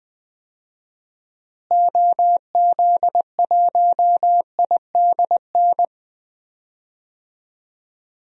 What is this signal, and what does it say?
Morse code, heard as a keyed tone, about 20 wpm, 700 Hz. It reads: OZ1IDN
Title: DXCC-100 Call Signs - 6 Characters - Repeated 3x - 20wpm